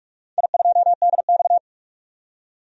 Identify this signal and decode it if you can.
Morse code, heard as a keyed tone, about 42 wpm, 700 Hz. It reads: I2DX
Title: DXCC-100 Call Signs - 4 Characters - 45wpm